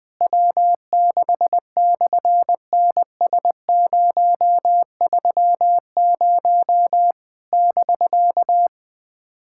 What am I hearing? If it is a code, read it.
Morse code, heard as a keyed tone, about 20 wpm, 700 Hz. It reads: W6/NS030 <BK>
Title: SOTA QSO Practice 20wpm